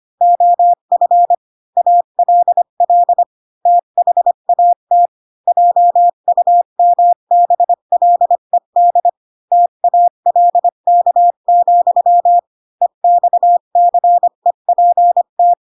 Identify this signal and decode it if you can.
Morse code, heard as a keyed tone, about 25 wpm, 700 Hz. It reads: OF ALL THAT JUMBLED TALK, EXCEPT
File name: Pinocchio / Pinocchio ch7 20 25 ik1hge